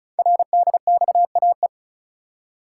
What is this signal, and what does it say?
Morse code, heard as a keyed tone, about 36 wpm, 700 Hz. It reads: RDXAE
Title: Code Groups - 5 Letters - 35wpm